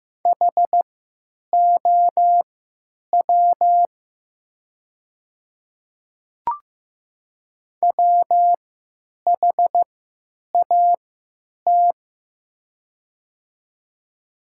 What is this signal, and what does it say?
Morse code, heard as a keyed tone, about 15 wpm, spaced as for 9 wpm, 700 Hz. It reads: HOW WHAT
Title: 3x Character Spacing - Top 300 Words - 15wpm